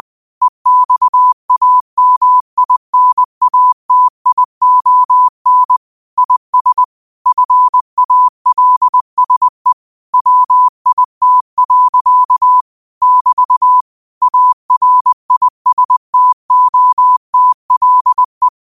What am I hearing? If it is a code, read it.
Morse code, heard as a keyed tone, about 20 wpm, 1000 Hz. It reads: EXAMINATION IS FALSE WIT. <BT> ARISTOTLE
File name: qotdfiles / 20wpm